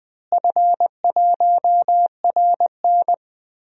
Morse code, heard as a keyed tone, about 20 wpm, 700 Hz. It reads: F1RN
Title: DXCC-100 Call Signs - 4 Characters - Repeated 2x - 20wpm